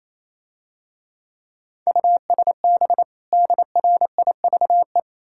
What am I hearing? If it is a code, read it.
Morse code, heard as a keyed tone, about 28 wpm, 700 Hz. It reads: USB DRIVE
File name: Easy Words - Speed-Racing with Hints - 22wpm